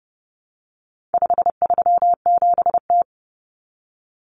Morse code, heard as a keyed tone, about 30 wpm, 700 Hz. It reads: 537T
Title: Common Cut Numbers - 4-Digit 30wpm